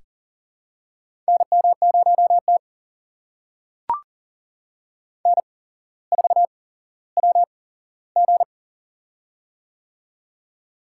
Morse code, heard as a keyed tone, about 40 wpm, spaced as for 17 wpm, 700 Hz. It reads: NM0T N4WG
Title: USA Call Signs - 4 Characters - 8x Character Spacing - 40wpm